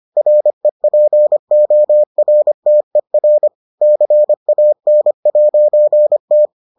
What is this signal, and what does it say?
Morse code, heard as a keyed tone, about 25 wpm, 600 Hz. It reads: REPORTER CAN'T